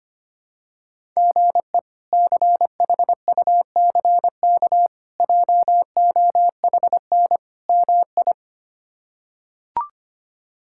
Morse code, heard as a keyed tone, about 25 wpm, 700 Hz. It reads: GE CHUCK JOHN MI
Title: K1USN SST QSO Practice - 25wpm